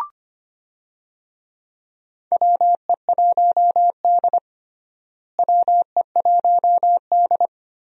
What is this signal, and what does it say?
Morse code, heard as a keyed tone, about 25 wpm, 700 Hz. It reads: WE1D WE1D
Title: USA Call Signs - 4 Characters - Repeated 3x - 25wpm